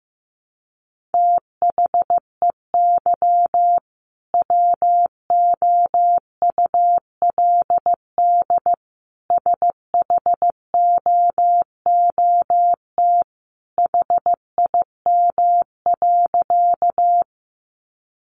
Morse code, heard as a keyed tone, about 15 wpm, 700 Hz. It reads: THEY WOULD SHOOT HIM.
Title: Sentences from Top 1200 Words 15wpm